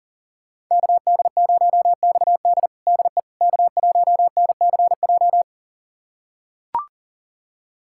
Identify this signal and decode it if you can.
Morse code, heard as a keyed tone, about 40 wpm, 700 Hz. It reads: KD0XD DE K1NCJ
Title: USA Callsign DE Callsign - 5 Characters - 40wpm